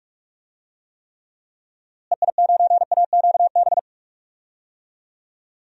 Morse code, heard as a keyed tone, about 48 wpm, 700 Hz. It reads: EI9AQB
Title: DXCC-100 Call Signs - 6 Characters - Repeated 2x - 45wpm